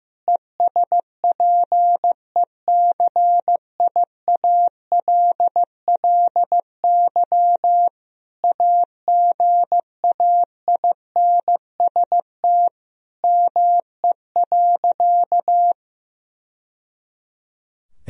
Morse code, heard as a keyed tone, about 15 wpm, 700 Hz. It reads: ESPECIALLY AGAINST ME.
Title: Sentences from Top 1000 Words - Review Part 4 - 15wpm